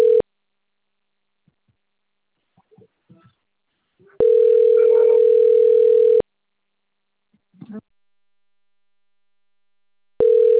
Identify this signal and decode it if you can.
Ringback tone (North America), 440 + 480 Hz, with speech.
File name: rg-1600-unknown- - - .37166